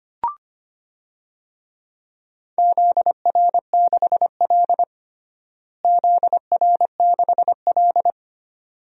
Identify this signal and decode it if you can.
Morse code, heard as a keyed tone, about 25 wpm, 700 Hz. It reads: ZR6L ZR6L
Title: DXCC-100 Call Signs - 4 Characters - Repeated 2x - 25wpm